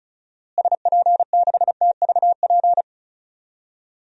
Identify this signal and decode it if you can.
Morse code, heard as a keyed tone, about 36 wpm, 700 Hz. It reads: SP6TVP
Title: DXCC-100 Call Signs - 6 Characters - Repeated 2x - 35wpm